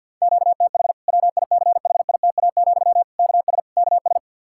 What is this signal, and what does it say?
Morse code, heard as a keyed tone, about 53 wpm, 700 Hz. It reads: QTH WICHITA, KS KS
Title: Practice QSO - Exchange 1 - Beginning - 50wpm